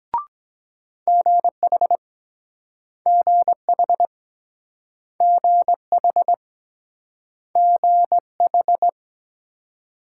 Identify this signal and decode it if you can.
Morse code, heard as a keyed tone, about 23 wpm, 700 Hz. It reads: GH GH GH GH